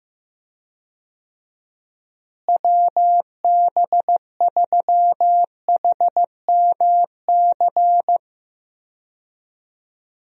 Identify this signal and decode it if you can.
Morse code, heard as a keyed tone, about 15 wpm, 700 Hz. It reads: WB3HMC
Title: USA Call Signs - 6 Characters - Repeated 2x - 15wpm